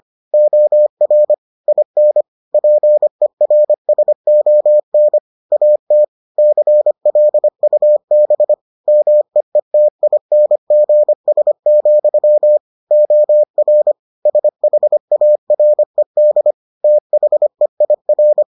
Morse code, heard as a keyed tone, about 25 wpm, 600 Hz. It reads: OR IN PERSON AT CLUB MEETINGS, OR SHARED THEIR